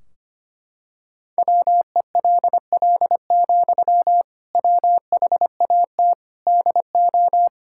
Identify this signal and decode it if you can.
Morse code, heard as a keyed tone, about 25 wpm, 700 Hz. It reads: WELL, WHAT DO
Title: Sentences from Top 1600 Words 25wpm